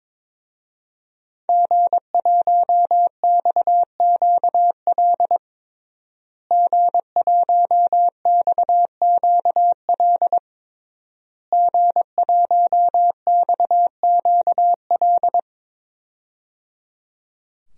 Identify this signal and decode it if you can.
Morse code, heard as a keyed tone, about 22 wpm, 700 Hz. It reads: G1XQL G1XQL G1XQL
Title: DXCC-100 Call Signs - 5 Characters - Repeated 3x - 22wpm